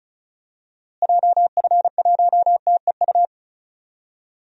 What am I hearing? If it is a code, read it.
Morse code, heard as a keyed tone, about 36 wpm, 700 Hz. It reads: JF1TEU
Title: DXCC-100 Call Signs - 6 Characters - Repeated 2x - 35wpm